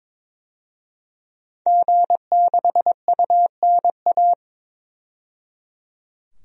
Morse code, heard as a keyed tone, about 22 wpm, 700 Hz. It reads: G6UNA
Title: DXCC-100 Call Signs - 5 Characters - 22wpm